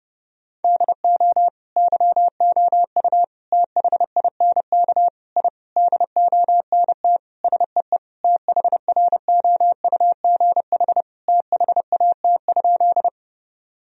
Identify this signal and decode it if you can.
Morse code, heard as a keyed tone, about 30 wpm, 700 Hz. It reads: DO YOU THINK I DONT SEE THROUGH THAT?
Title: Sentences from Top 1000 Words - Review Part 1 - 30wpm